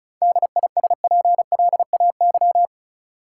Morse code, heard as a keyed tone, about 36 wpm, 700 Hz. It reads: DISPLAY
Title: Sentences from Top 1200 Words - New Word Drill - 35wpm